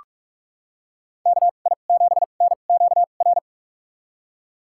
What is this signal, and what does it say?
Morse code, heard as a keyed tone, about 42 wpm, spaced as for 31 wpm, 700 Hz. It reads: KI7NQR
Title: USA Call Signs - 6 Characters - 2x Character Spacing - 45wpm